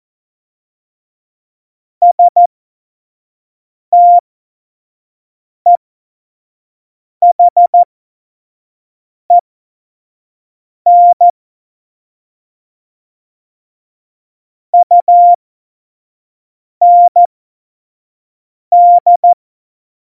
Morse code, heard as a keyed tone, about 14 wpm, spaced as for 5 wpm, 700 Hz. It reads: STEHEN UND
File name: Morsekurs / Witze / Witz 24 April 2018, 25 F70